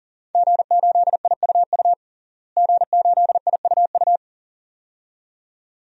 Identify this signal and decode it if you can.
Morse code, heard as a keyed tone, about 40 wpm, 700 Hz. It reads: G8IUU G8IUU
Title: DXCC-100 Call Signs - 5 Characters - Repeated 2x - 40wpm